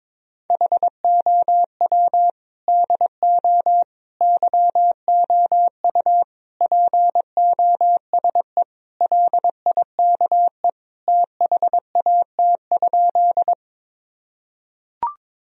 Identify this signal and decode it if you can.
Morse code, heard as a keyed tone, about 22 wpm, 700 Hz. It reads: HOW DO YOU POSE LIKE THAT?
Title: Sentences from Top 1000 Words - Review Part 2 - 22wpm